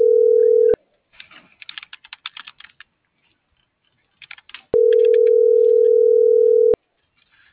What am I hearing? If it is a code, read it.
Ringback tone (North America), 440 + 480 Hz.
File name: CENTRAL MULTIMODAL / out viridiana.juarez - .83239